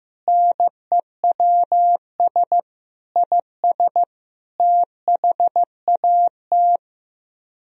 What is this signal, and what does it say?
Morse code, heard as a keyed tone, about 15 wpm, 700 Hz. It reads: NEWS IS THAT
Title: Speed-Racing - Sets of 5 Words 15wpm